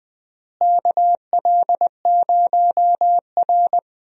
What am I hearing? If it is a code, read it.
Morse code, heard as a keyed tone, about 20 wpm, 700 Hz. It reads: KL0R
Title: USA Call Signs - 4 Characters - Repeated 2x - 20wpm